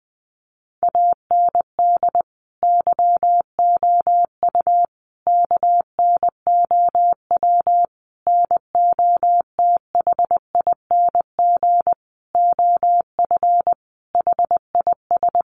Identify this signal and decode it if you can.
Morse code, heard as a keyed tone, about 20 wpm, 700 Hz. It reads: AND YOU KNOW NOTHING OF HIS
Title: Sentences from Top 1200 Words 20wpm